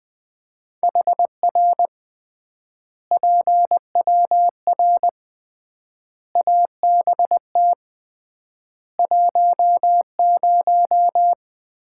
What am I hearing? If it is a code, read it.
Morse code, heard as a keyed tone, about 20 wpm, 700 Hz. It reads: HR PWR ABT 10
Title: Common QSO Phrases - 3x Word-Spacing - 20wpm